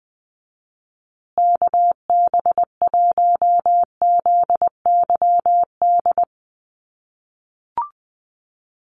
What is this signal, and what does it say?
Morse code, heard as a keyed tone, about 20 wpm, 700 Hz. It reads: KB1ZYD
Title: USA Call Signs Repeated 3x 20wpm